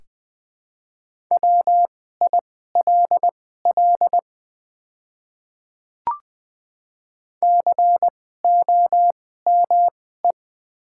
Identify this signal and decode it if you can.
Morse code, heard as a keyed tone, about 21 wpm, spaced as for 15 wpm, 700 Hz. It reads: WILL COME
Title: 2x Character Spacing - Top 100 Words - 20wpm